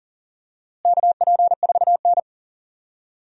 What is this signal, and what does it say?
Morse code, heard as a keyed tone, about 37 wpm, 700 Hz. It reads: KP4N
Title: DXCC-100 Call Signs - 4 Characters - Repeated 3x - 40wpm